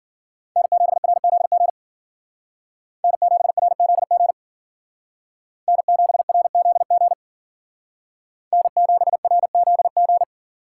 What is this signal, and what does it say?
Morse code, heard as a keyed tone, about 50 wpm, 700 Hz. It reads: N7RZG N7RZG N7RZG N7RZG
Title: Speed Racing - USA 1x3 Call Signs 40wpm